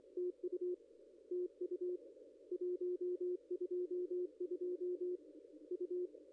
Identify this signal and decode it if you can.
Morse code, heard as a keyed tone, about 24 wpm, 360 Hz, with radio noise. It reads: TU TU 122 U
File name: contests / ss-cw-k5tr-op-n5rz -0600